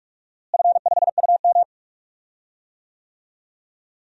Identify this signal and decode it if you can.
Morse code, heard as a keyed tone, about 46 wpm, 700 Hz. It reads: F5UM
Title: DXCC-100 Call Signs - 4 Characters - Repeated 3x - 45wpm